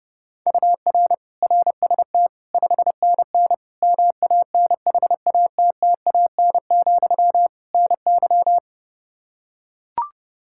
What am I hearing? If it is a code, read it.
Morse code, heard as a keyed tone, about 32 wpm, 700 Hz. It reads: UR RST 5NN MANHATTAN, NY
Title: Practice QSO - Exchange 1 - Beginning - 30wpm